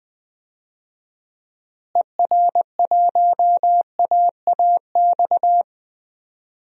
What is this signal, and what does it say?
Morse code, heard as a keyed tone, about 20 wpm, 700 Hz. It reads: ER1AAX
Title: DXCC-100 Call Signs - 6 Characters - Repeated 2x - 20wpm